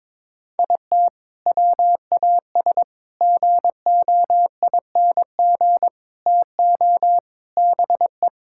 Morse code, heard as a keyed tone, about 22 wpm, 700 Hz. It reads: IT WAS GOING TO BE
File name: practice / 22wpm / Sentences with 4-Word Phrases - 22wpm